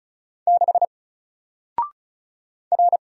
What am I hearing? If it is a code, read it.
Morse code, heard as a keyed tone, about 34 wpm, 700 Hz. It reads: 6 R
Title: Single Character (Letter:Number:Punctuation:Prosign) 35wpm